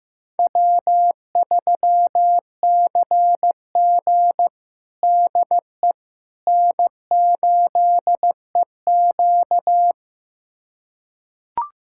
Morse code, heard as a keyed tone, about 15 wpm, 700 Hz. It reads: W3CG DE N8EQ
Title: USA Callsign DE Callsign - 4 Characters - 15wpm